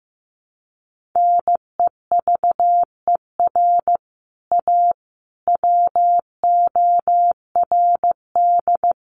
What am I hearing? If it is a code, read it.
Morse code, heard as a keyed tone, about 15 wpm, 700 Hz. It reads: NEVER A WORD
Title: Sentences from Top 1300 Words 15wpm